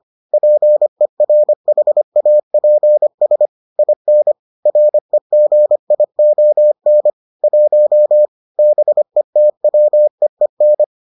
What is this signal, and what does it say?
Morse code, heard as a keyed tone, about 25 wpm, 600 Hz. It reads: PERHAPS IN REGION 1 BETWEEN